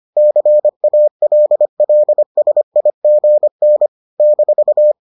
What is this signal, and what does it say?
Morse code, heard as a keyed tone, about 25 wpm, 600 Hz. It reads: CALLSIGN -